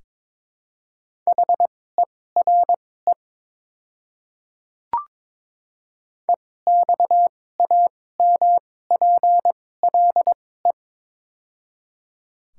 Morse code, heard as a keyed tone, about 22 wpm, spaced as for 16 wpm, 700 Hz. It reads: HERE EXAMPLE
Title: 2x Character Spacing - Top 500 Words - 22wpm